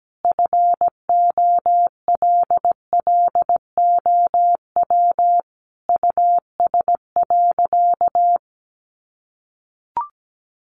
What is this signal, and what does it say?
Morse code, heard as a keyed tone, about 17 wpm, 700 Hz. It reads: FOLLOW US.
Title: Sentences from Top 1500 Words 17wpm